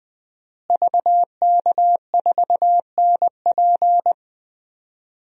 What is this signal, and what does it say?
Morse code, heard as a keyed tone, about 20 wpm, 700 Hz. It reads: VK4NP
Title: DXCC-100 Call Signs - 5 Characters - Repeated 3x - 20wpm